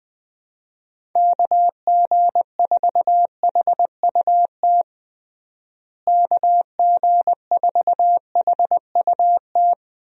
Morse code, heard as a keyed tone, about 21 wpm, 700 Hz. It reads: KG4HUT KG4HUT
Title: USA Call Signs - 6 Characters - Repeated 3x - 20wpm